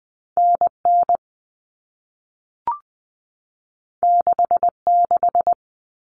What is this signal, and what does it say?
Morse code, heard as a keyed tone, about 20 wpm, 700 Hz. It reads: NN 66
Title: Common Cut Numbers - 2-Digit 20wpm